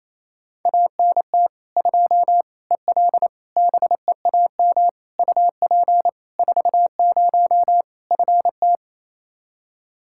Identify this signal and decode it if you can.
Morse code, heard as a keyed tone, about 27 wpm, 700 Hz. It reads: ANT 2 EL BEAM UP 40 FT
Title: Practice QSO - Exchange 2 - Info - 28wpm